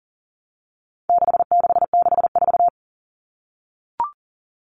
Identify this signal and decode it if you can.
Morse code, heard as a keyed tone, about 37 wpm, 700 Hz. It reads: 6664
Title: Numbers - 4-Digit 40wpm